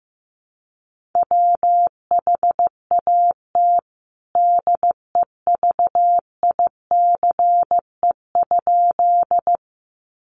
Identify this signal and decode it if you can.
Morse code, heard as a keyed tone, about 15 wpm, 700 Hz. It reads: WHAT DEVICE?
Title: Sentences from Top 1500 Words 15wpm